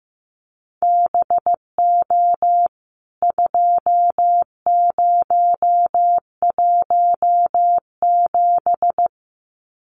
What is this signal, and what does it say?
Morse code, heard as a keyed tone, about 15 wpm, 700 Hz. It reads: BO 2017
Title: CWT Exchange 15wpm